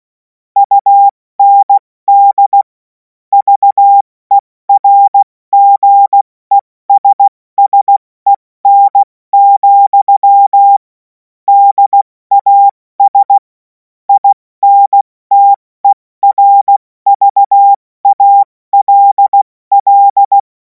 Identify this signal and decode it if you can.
Morse code, heard as a keyed tone, about 16 wpm, 800 Hz. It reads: UND VERGESSEN, DAS INTERVALL
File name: Morsekurs / Witze / Witz 28. April 2026 70 F80